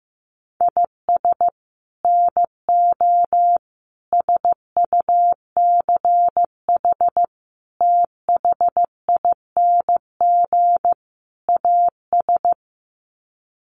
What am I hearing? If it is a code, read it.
Morse code, heard as a keyed tone, about 15 wpm, 700 Hz. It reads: IS NO SUCH THING AS
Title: Sets of 5 Words Encore 15wpm